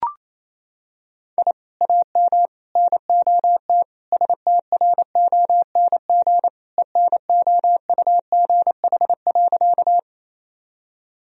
Morse code, heard as a keyed tone, about 27 wpm, 700 Hz. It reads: I AM NOT STRONG ENOUGH.
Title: Sentences from Top 1000 Words - Review Part 3 - 28wpm